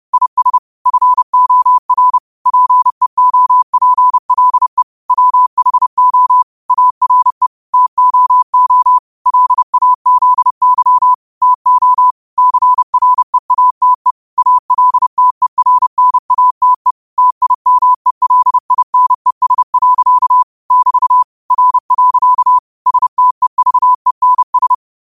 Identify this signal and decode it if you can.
Morse code, heard as a keyed tone, about 31 wpm, 1000 Hz. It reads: IS FOR PEOPLE WHO ARE TOO LAZY TO CREATE ALTERNATE TIMELINES. <BT> R. STEVENS
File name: qotdfiles / 30wpm